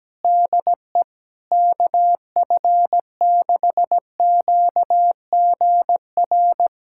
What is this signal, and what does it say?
Morse code, heard as a keyed tone, about 17 wpm, 700 Hz. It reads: DE KF6QGR
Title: USA Callsign DE Callsign 17wpm